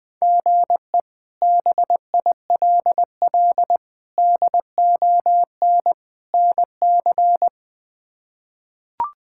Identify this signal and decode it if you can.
Morse code, heard as a keyed tone, about 20 wpm, 700 Hz. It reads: GE BILL DON NC
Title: K1USN SST QSO Practice - 20wpm